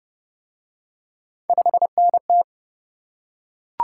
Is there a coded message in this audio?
Yes: Morse code from 1.5 to 2.4 s, heard as a keyed tone, about 28 wpm, 700 Hz. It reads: 5NT